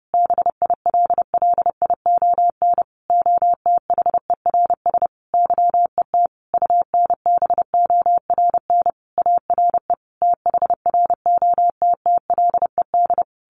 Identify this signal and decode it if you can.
Morse code, heard as a keyed tone, about 28 wpm, 700 Hz. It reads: BILLION OTHERS YET UNBORN ARE THROTTLED
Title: A Sound Of Thunder 30wpm